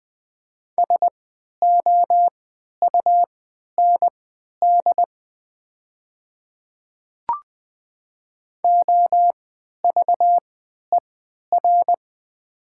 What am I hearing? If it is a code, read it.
Morse code, heard as a keyed tone, about 20 wpm, spaced as for 11 wpm, 690 Hz. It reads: SOUND OVER
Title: 3x Character Spacing - Top 100 Words - 20wpm